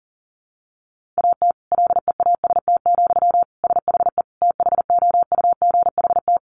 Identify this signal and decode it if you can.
Morse code, heard as a keyed tone, about 40 wpm, 700 Hz. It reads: AT LEAST, SHE THOUGHT
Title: Sentences from Top 1400 Words 40wpm